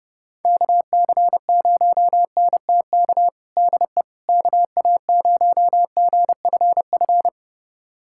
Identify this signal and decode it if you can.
Morse code, heard as a keyed tone, about 30 wpm, 700 Hz. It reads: KC0NTK DE KA0GFF
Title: USA Callsign DE Callsign 30wpm